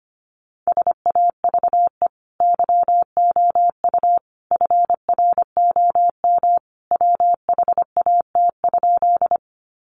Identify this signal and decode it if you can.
Morse code, heard as a keyed tone, about 25 wpm, 700 Hz. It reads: SAVE YOU FROM WHAT?
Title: Sentences from Top 1100 Words 25wpm